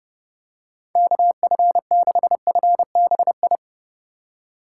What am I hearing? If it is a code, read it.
Morse code, heard as a keyed tone, about 32 wpm, 700 Hz. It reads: KF6FBI
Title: USA Call Signs - 6 Characters - 2x Character Spacing - 30wpm